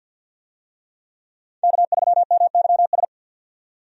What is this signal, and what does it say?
Morse code, heard as a keyed tone, about 47 wpm, 700 Hz. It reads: K3MYS
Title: USA Call Signs - 5 Characters - 50wpm